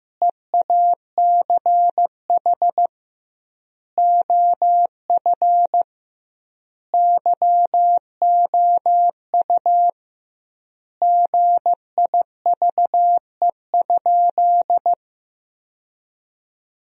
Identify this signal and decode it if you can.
Morse code, heard as a keyed tone, about 15 wpm, 700 Hz. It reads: EACH OF YOU GIVE?
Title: 2x Word-Spacing - Sentences from Top 200 Words - 15wpm